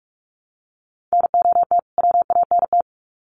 Morse code, heard as a keyed tone, about 44 wpm, 700 Hz. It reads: NOT WANT
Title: Sets of 2 Words 45wpm